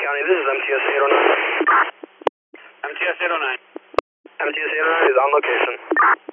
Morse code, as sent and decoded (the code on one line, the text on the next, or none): none